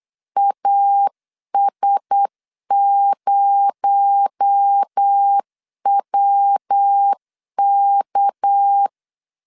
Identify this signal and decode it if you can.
Morse code, heard as a keyed tone, about 8 wpm, 800 Hz. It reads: AS0WK
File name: CW / DET / 1 6 1500C S C 8wpm 5219